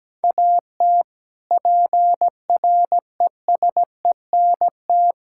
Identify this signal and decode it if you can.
Morse code, heard as a keyed tone, about 17 wpm, 700 Hz. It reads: AT PRESENT
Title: Sentences from Top 1000 Words - Review Part 5 - 17wpm